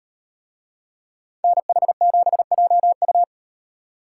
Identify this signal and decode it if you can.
Morse code, heard as a keyed tone, about 36 wpm, 700 Hz. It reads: NH7JU